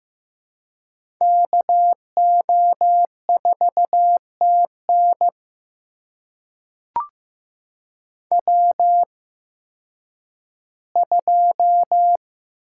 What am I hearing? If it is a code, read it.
Morse code, heard as a keyed tone, about 15 wpm, spaced as for 10 wpm, 700 Hz. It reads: KO4TN W2
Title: USA Call Signs - 8x Character Spacing - 15wpm